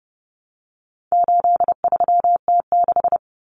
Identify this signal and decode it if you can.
Morse code, heard as a keyed tone, about 30 wpm, 700 Hz. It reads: 83T6
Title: Common Cut Numbers - 4-Digit 30wpm